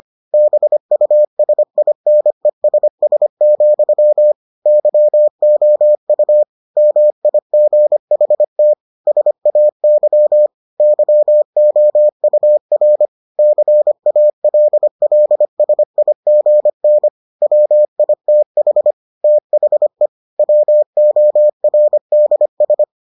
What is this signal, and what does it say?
Morse code, heard as a keyed tone, about 25 wpm, 600 Hz. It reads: BUSINESS, YOU MIGHT SAY YOUR CALLSIGN WITH THE WORDS